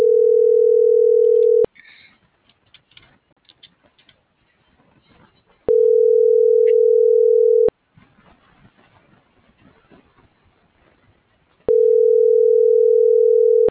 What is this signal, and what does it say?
Ringback tone (North America), 440 + 480 Hz.